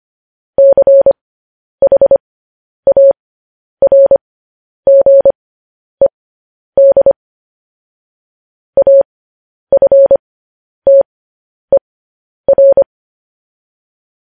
Morse code, heard as a keyed tone, about 25 wpm, spaced as for 10 wpm, 550 Hz. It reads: CHARGED AFTER
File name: podcast / News headlines-1x-25x10